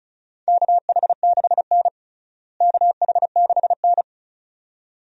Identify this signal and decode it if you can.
Morse code, heard as a keyed tone, about 34 wpm, 700 Hz. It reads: KH6N KH6N
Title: DXCC-100 Call Signs - 4 Characters - Repeated 2x - 35wpm